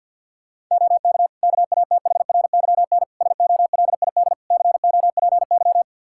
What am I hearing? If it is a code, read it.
Morse code, heard as a keyed tone, about 52 wpm, 700 Hz. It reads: OK KATHRYN SOLID COPY
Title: K1USN SST QSO Practice 50wpm